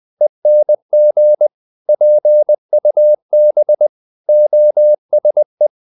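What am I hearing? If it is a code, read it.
Morse code, heard as a keyed tone, about 20 wpm, 600 Hz. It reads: ENG PUB OSE
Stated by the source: Morse code practice of short Danish words at 20 wpm, from CW med Gnister.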